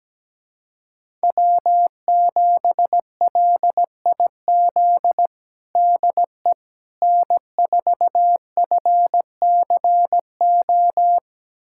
Morse code, heard as a keyed tone, about 17 wpm, 700 Hz. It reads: W7LIZ DE N4FCO
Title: USA Callsign DE Callsign - 5 Characters - 17wpm